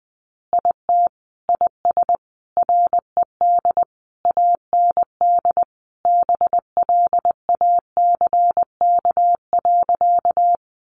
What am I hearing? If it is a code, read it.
Morse code, heard as a keyed tone, about 20 wpm, 700 Hz. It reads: IT IS RED AND BLACK.
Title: Easy Phrases 20wpm